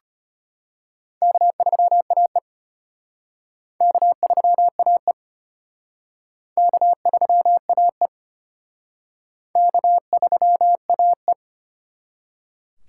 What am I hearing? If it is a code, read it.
Morse code, heard as a keyed tone, about 33 wpm, 700 Hz. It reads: K3AE K3AE K3AE K3AE